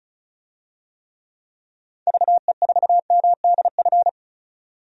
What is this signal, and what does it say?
Morse code, heard as a keyed tone, about 36 wpm, 700 Hz. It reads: VE4MDF
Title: DXCC-100 Call Signs - 6 Characters - 35wpm